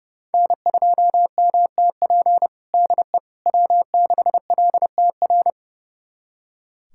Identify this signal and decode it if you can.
Morse code, heard as a keyed tone, about 30 wpm, 700 Hz. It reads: N2MTP DE W6LTR
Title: USA Callsign DE Callsign - 5 Characters - 30wpm